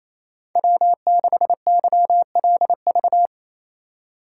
Morse code, heard as a keyed tone, about 28 wpm, 700 Hz. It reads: W6YLV
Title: USA Call Signs - 5 Characters - 8x Character Spacing - 28wpm